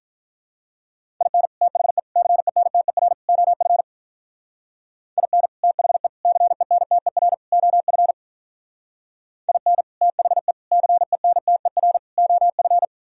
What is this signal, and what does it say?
Morse code, heard as a keyed tone, about 44 wpm, 700 Hz. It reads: IN THE CENTER OF IN THE CENTER OF IN THE CENTER OF